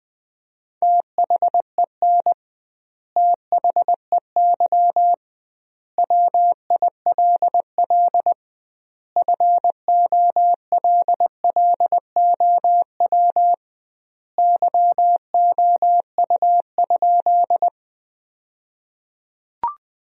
Morse code, heard as a keyed tone, about 20 wpm, 700 Hz. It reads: THEN THEY WILL FOLLOW YOU?
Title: 2x Word-Spacing - Sentences from Top 200 Words - 20wpm